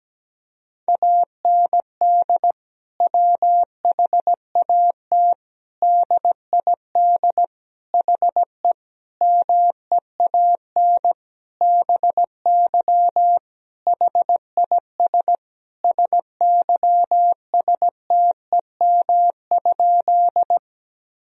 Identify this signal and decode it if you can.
Morse code, heard as a keyed tone, about 17 wpm, 700 Hz. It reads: AND WHAT DID HE MEAN BY HIS SYSTEM?
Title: Sentences from Top 1000 Words - Review Part 3 - 17wpm